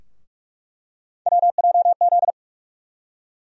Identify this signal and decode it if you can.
Morse code, heard as a keyed tone, about 46 wpm, 700 Hz. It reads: WJZ
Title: USA Call Sign Postfixes - 3 Letters 45wpm